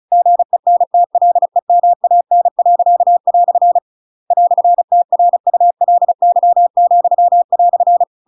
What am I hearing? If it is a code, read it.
Morse code, heard as a keyed tone, about 34 wpm, 700 Hz. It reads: GENTLEMAN." "TRULY,"
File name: Pinocchio / Pinocchio ch8 35 35 ik1hge